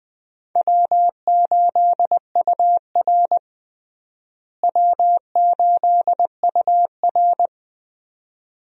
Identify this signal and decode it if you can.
Morse code, heard as a keyed tone, about 20 wpm, 700 Hz. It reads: W8UR W8UR
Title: USA Call Signs - 4 Characters - Repeated 2x - 20wpm